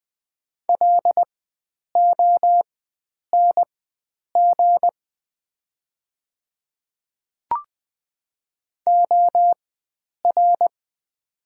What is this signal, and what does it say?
Morse code, heard as a keyed tone, about 20 wpm, spaced as for 9 wpm, 700 Hz. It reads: LONG OR
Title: 4x Character Spacing - Top 100 Words - 20wpm